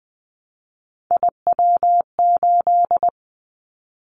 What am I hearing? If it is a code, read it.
Morse code, heard as a keyed tone, about 20 wpm, 700 Hz. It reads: IW8
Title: Callsign Prefixes for Top 25 DXCC - Repeated 3x - 20wpm